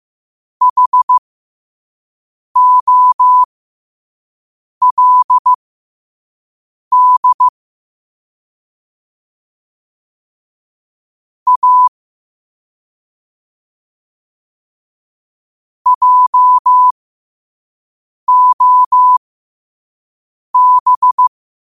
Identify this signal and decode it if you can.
Morse code, heard as a keyed tone, about 15 wpm, spaced as for 5 wpm, 1000 Hz. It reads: HOLD A JOB
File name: qotdfiles / 5wpm